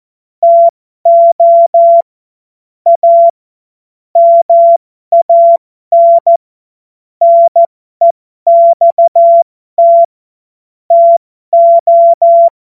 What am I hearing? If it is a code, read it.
Morse code, heard as a keyed tone, about 14 wpm, spaced as for 12 wpm, 680 Hz. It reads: TO A MAN NEXT TO
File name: Morsekurs / Witze